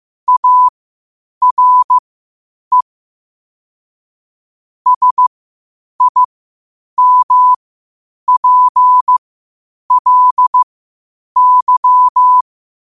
Morse code, heard as a keyed tone, about 15 wpm, spaced as for 8 wpm, 1000 Hz. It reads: ARE SIMPLY
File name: qotdfiles / 8wpm